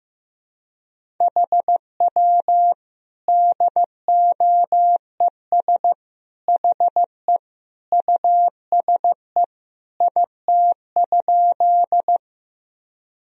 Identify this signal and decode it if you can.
Morse code, heard as a keyed tone, about 15 wpm, 700 Hz. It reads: HW DOES HE USE IT?
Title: Sentences from Top 200 Words and Abbreviations 15wpm